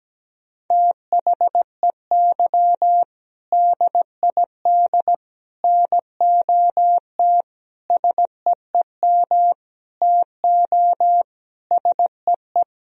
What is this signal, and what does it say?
Morse code, heard as a keyed tone, about 17 wpm, 700 Hz. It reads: THEY DID NOT SEEM TO SEE
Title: Sentences from Top 1000 Words - Review Part 1 - 17wpm